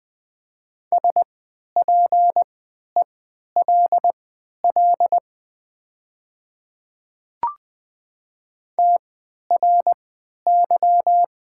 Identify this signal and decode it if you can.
Morse code, heard as a keyed tone, about 20 wpm, spaced as for 11 wpm, 700 Hz. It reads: SPELL TRY
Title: 3x Character Spacing - Top 500 Words - 20wpm